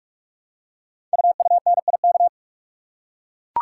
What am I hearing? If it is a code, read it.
Morse code, heard as a keyed tone, about 44 wpm, 690 Hz. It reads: UUNIK
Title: Code Groups - 5 Letters - 45wpm